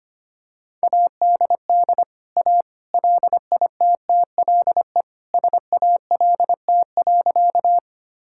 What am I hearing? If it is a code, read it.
Morse code, heard as a keyed tone, about 25 wpm, 700 Hz. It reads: ADD A LITTLE SALT.
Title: Sentences from Top 1000 Words - Review Part 5 - 25wpm